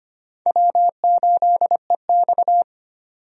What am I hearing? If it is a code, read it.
Morse code, heard as a keyed tone, about 25 wpm, 700 Hz. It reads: W8EX
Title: USA Call Signs - 4 Characters - 25wpm